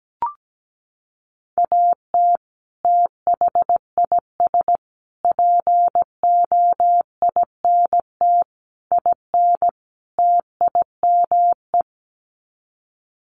Morse code, heard as a keyed tone, about 17 wpm, 700 Hz. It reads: AT THIS POINT IN TIME
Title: Sets of 5 Words Encore 17wpm